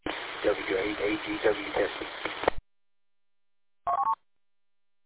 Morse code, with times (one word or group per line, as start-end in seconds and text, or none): none